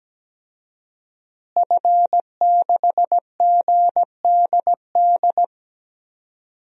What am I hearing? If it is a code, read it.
Morse code, heard as a keyed tone, about 17 wpm, 700 Hz. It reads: F6GDD
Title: DXCC-100 Call Signs - 5 Characters - Repeated 3x - 17wpm